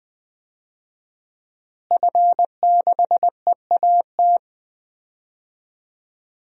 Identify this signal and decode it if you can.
Morse code, heard as a keyed tone, about 21 wpm, 700 Hz. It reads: F6EAT
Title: DXCC-100 Call Signs - 5 Characters - Repeated 3x - 20wpm